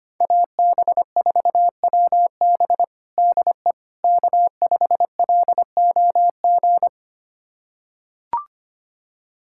Morse code, heard as a keyed tone, about 25 wpm, 700 Hz. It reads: AB4WB DE K5LOG
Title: USA Callsign DE Callsign - 5 Characters - 25wpm